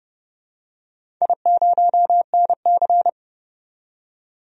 Morse code, heard as a keyed tone, about 31 wpm, 700 Hz. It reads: I0NC
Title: DXCC-100 Call Signs - 4 Characters - 30wpm